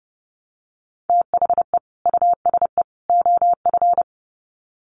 Morse code, heard as a keyed tone, about 30 wpm, 700 Hz. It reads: THE USE OF
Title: Sets of 3 Words 30wpm